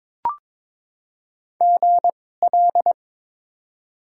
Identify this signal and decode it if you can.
Morse code, heard as a keyed tone, about 21 wpm, spaced as for 16 wpm, 700 Hz. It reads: GL
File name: Top 90 QSO Elements - 2x Character Spacing - 22wpm